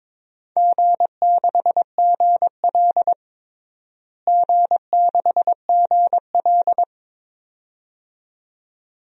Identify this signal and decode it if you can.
Morse code, heard as a keyed tone, about 21 wpm, 700 Hz. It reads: G6GL G6GL
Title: DXCC-100 Call Signs - 4 Characters - Repeated 2x - 22wpm